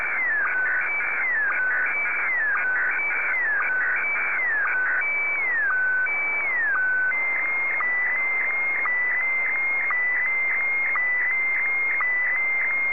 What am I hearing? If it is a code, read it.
SSTV (slow-scan television), Scottie DX.